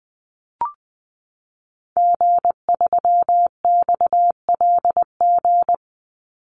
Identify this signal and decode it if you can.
Morse code, heard as a keyed tone, about 20 wpm, 700 Hz. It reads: G3XLG
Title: Call Signs - 20wpm